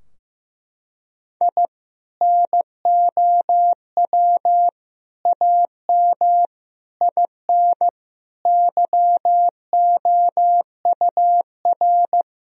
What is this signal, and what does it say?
Morse code, heard as a keyed tone, about 15 wpm, 700 Hz. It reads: I NOW AM IN YOUR
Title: Sentences from Top 1000 Words - Review Part 2 - 15wpm